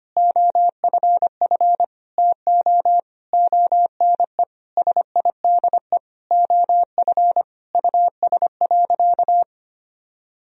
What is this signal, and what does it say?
Morse code, heard as a keyed tone, about 25 wpm, 700 Hz. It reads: OFF TO ONE SIDE OF US.
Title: Sentences from Top 1000 Words - Review Part 3 - 25wpm